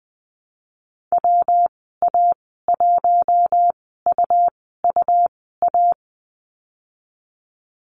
Morse code, heard as a keyed tone, about 20 wpm, spaced as for 15 wpm, 700 Hz. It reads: WA1UUA
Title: 147 Lesson 24g - Letter y - Call Signs 20wpm